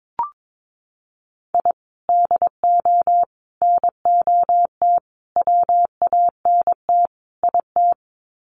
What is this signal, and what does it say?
Morse code, heard as a keyed tone, about 22 wpm, 700 Hz. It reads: I DO NOT WANT IT
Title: Sets of 5 Words Encore2 22wpm